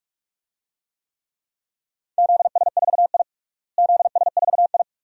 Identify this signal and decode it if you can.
Morse code, heard as a keyed tone, about 46 wpm, 700 Hz. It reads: ZS4I ZS4I
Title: DXCC-100 Call Signs - 4 Characters - Repeated 2x - 45wpm